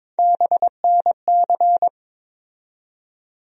Morse code, heard as a keyed tone, about 22 wpm, 700 Hz. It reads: BNC
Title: USA Call Sign Postfixes - 3 Letters 22wpm